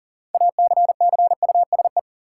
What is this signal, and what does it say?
Morse code, heard as a keyed tone, about 40 wpm, 700 Hz. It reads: ACCUSE
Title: Six Letter Words - 40wpm